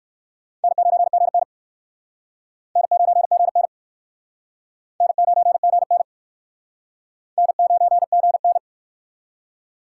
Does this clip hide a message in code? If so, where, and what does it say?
Yes: Morse code from 0.6 to 8.6 s, heard as a keyed tone, about 60 wpm, 700 Hz. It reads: N9GN N9GN N9GN N9GN